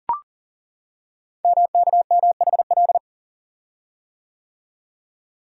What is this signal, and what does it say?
Morse code, heard as a keyed tone, about 41 wpm, 690 Hz. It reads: MKMHL
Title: Code Groups - 5 Letters - 40wpm